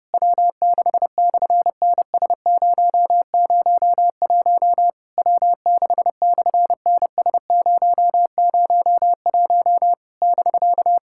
Morse code, heard as a keyed tone, about 31 wpm, 700 Hz. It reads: W6/NS001 W6/NS001 <BK>
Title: SOTA QSO Practice 30wpm